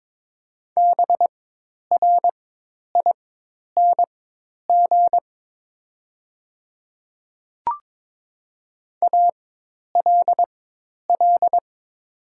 Morse code, heard as a keyed tone, about 23 wpm, spaced as for 10 wpm, 700 Hz. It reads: BRING ALL
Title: 4x Character Spacing - Top 500 Words - 22wpm